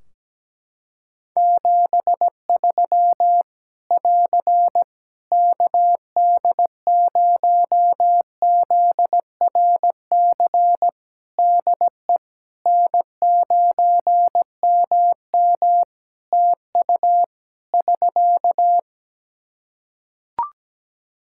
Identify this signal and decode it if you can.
Morse code, heard as a keyed tone, about 17 wpm, 700 Hz. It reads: 73 <AR> KD0ZRC DE N9MM TU <SK>
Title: Practice QSO - Exchange 4 - Ending - 17wpm